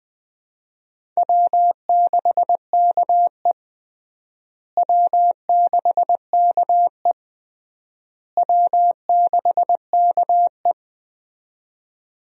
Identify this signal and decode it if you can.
Morse code, heard as a keyed tone, about 20 wpm, 700 Hz. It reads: W6KE W6KE W6KE
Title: USA Call Signs - 4 Characters - Repeated 3x - 20wpm